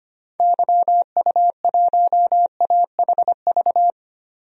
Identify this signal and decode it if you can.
Morse code, heard as a keyed tone, about 25 wpm, 700 Hz. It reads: YU1AHV
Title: DXCC-100 Call Signs - 6 Characters - 25wpm